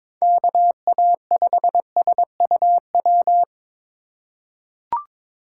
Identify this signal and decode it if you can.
Morse code, heard as a keyed tone, about 22 wpm, 700 Hz. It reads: KA5SUW
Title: USA Call Signs - 6 Characters - 4x Character Spacing - 22wpm